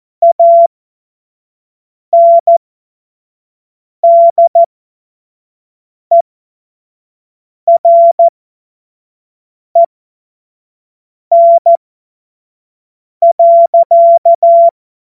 Morse code, heard as a keyed tone, about 14 wpm, spaced as for 5 wpm, 680 Hz. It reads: ANDEREN.
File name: Morsekurs / Witze / Witz 04. Januar 2022, 25 F70